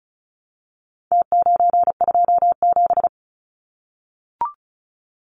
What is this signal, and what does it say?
Morse code, heard as a keyed tone, about 34 wpm, 700 Hz. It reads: T927
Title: Numbers - Contest Style - 4-Digit 35wpm